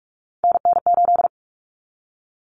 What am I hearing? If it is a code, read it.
Morse code, heard as a keyed tone, about 44 wpm, 700 Hz. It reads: NN8
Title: Common Cut Numbers - 3-Digit 45wpm